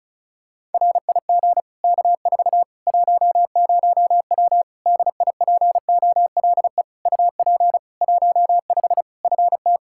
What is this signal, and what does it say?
Morse code, heard as a keyed tone, about 36 wpm, 700 Hz. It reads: RIG K4 10W DIPOLE UP 15 FT
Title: Practice QSO - Exchange 2 - Info - 35wpm